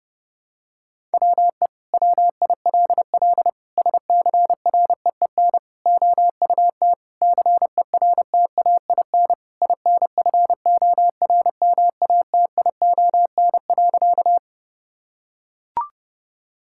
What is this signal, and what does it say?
Morse code, heard as a keyed tone, about 30 wpm, 700 Hz. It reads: WE WILL SCREEN OUT CERTAIN INFORMATION.
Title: Sentences from Top 1000 Words - Review Part 6 - 30wpm